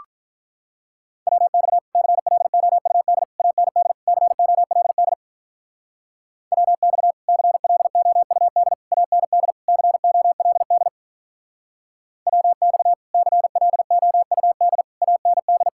Morse code, heard as a keyed tone, about 48 wpm, 700 Hz. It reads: WX CLOUD AND COLD WX CLOUD AND COLD WX CLOUD AND